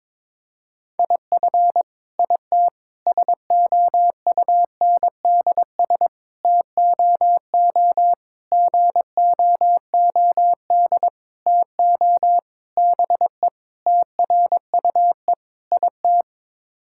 Morse code, heard as a keyed tone, about 22 wpm, 700 Hz. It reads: IF IT SOUNDS TOO GOOD TO BE TRUE IT
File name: Sets of 9 Words - 22wpm